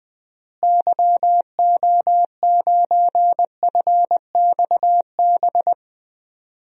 Morse code, heard as a keyed tone, about 20 wpm, 700 Hz. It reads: YO9FXB
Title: DXCC-100 Call Signs - 6 Characters - Repeated 3x - 20wpm